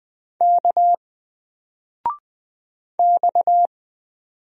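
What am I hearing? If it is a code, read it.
Morse code, heard as a keyed tone, about 20 wpm, 700 Hz. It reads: K X
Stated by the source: Single letter and number Morse drill at 20 wpm, from N4LCD.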